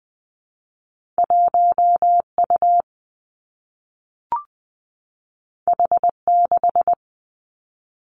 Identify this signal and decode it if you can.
Morse code, heard as a keyed tone, about 19 wpm, 700 Hz. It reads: 1 U H 6
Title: Two Challenging Letter-Number Combinations 20wpm